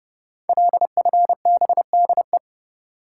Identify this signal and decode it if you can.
Morse code, heard as a keyed tone, about 30 wpm, 700 Hz. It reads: LFBDE
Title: Code Groups - 5 Letters - 30wpm